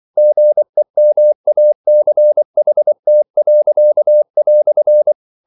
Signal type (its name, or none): Morse code